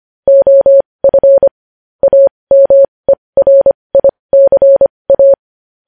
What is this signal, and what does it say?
Morse code, heard as a keyed tone, about 25 wpm, spaced as for 20 wpm, 550 Hz. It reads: OF AMERICA
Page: News Headlines in Morse Code at 20 WPM